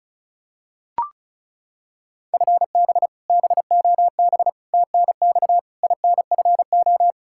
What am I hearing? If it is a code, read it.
Morse code, heard as a keyed tone, about 34 wpm, 700 Hz. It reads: FB BOB TNX INFO
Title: K1USN SST QSO Practice 35wpm